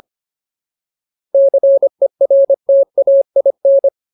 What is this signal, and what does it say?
Morse code, heard as a keyed tone, about 25 wpm, 550 Hz. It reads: CERTAIN
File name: Ultimate Word List / 25WPM Ultimate